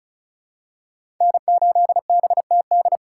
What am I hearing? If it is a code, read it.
Morse code, heard as a keyed tone, about 36 wpm, 700 Hz. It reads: N8BTD
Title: USA Call Signs - 5 Characters - 2x Character Spacing - 35wpm